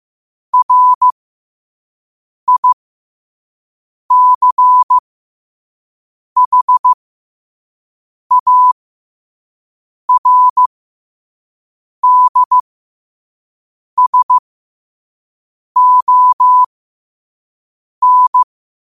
Morse code, heard as a keyed tone, about 15 wpm, spaced as for 5 wpm, 1000 Hz. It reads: RICHARDSON
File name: qotdfiles / 5wpm